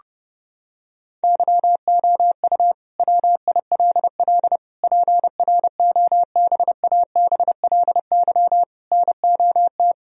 Morse code, heard as a keyed tone, about 32 wpm, 700 Hz. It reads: YOU WILL PROBABLY NOT
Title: Sentences from Top 1000 Words - Review Part 4 - 30wpm